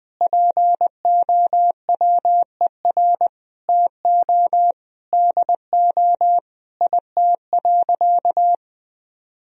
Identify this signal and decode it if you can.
Morse code, heard as a keyed tone, about 20 wpm, 700 Hz. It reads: POWER TO DO IT.
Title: Sentences from Top 1000 Words - Review Part 3 - 20wpm